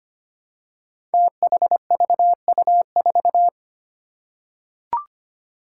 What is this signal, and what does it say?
Morse code, heard as a keyed tone, about 25 wpm, 700 Hz. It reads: THVU4
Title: Five Challenging Letter-Number Combinations 25wpm